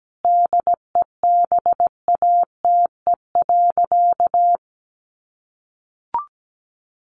Morse code, heard as a keyed tone, about 17 wpm, 700 Hz. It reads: DEBATE.
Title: Sentences from Top 1100 Words 17wpm